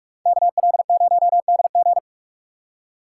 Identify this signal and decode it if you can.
Morse code, heard as a keyed tone, about 58 wpm, 700 Hz. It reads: KL0DG